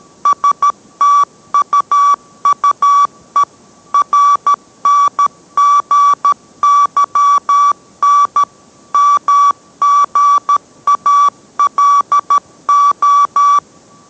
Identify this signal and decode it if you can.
Morse code, heard as a keyed tone, about 13 wpm, 1150 Hz. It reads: STUUE RNGYN MGALO